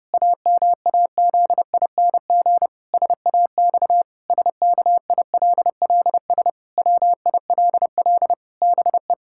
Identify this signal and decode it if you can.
Morse code, heard as a keyed tone, about 30 wpm, 700 Hz. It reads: AMAZING SAX SKILLS WILL BE
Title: Pangram Challenge v2 - 30wpm